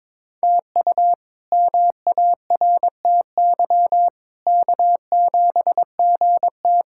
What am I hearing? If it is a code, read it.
Morse code, heard as a keyed tone, about 22 wpm, 700 Hz. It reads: TU MARTY K7GT
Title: K1USN SST QSO Practice - 22wpm